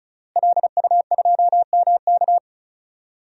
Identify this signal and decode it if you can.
Morse code, heard as a keyed tone, about 36 wpm, 700 Hz. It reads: LU2MK
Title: DXCC-100 Call Signs - 5 Characters - Repeated 3x - 35wpm